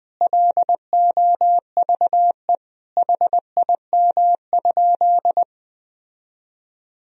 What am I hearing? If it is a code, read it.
Morse code, heard as a keyed tone, about 20 wpm, 700 Hz. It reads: LOVE HIM?
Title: Sentences from Top 1000 Words - Review Part 1 - 20wpm